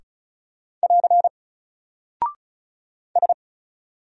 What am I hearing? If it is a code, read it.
Morse code, heard as a keyed tone, about 36 wpm, 700 Hz. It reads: <AR> S
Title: Single Character (Letter:Number:Punctuation:Prosign) Alt 35wpm